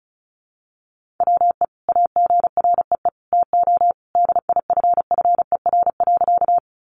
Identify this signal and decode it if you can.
Morse code, heard as a keyed tone, about 34 wpm, 700 Hz. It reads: WE AGREE TO DIFFER.
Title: Sentences from Top 1100 Words 35wpm